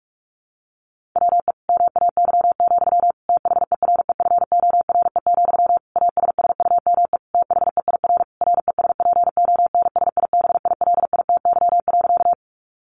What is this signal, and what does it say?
Morse code, heard as a keyed tone, about 44 wpm, 700 Hz. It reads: WE MAY, THEREFORE, ASSUME THEIR RESPONSIBILITY.
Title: Sentences from Top 1500 Words 45wpm